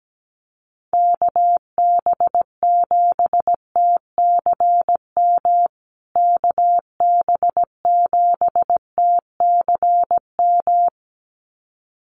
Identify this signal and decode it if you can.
Morse code, heard as a keyed tone, about 17 wpm, 700 Hz. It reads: KB7TCM KB7TCM
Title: USA Call Signs Repeated 2x 17wpm